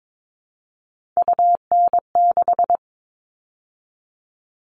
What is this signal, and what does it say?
Morse code, heard as a keyed tone, about 22 wpm, 700 Hz. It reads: UN6
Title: All Cut Numbers - 3-Digit 22wpm